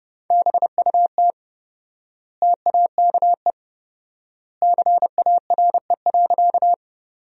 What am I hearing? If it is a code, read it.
Morse code, heard as a keyed tone, about 32 wpm, 700 Hz. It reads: BUT TAKE CARE.
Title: 4x Word-Spacing - Sentences from Top 300 Words - 30wpm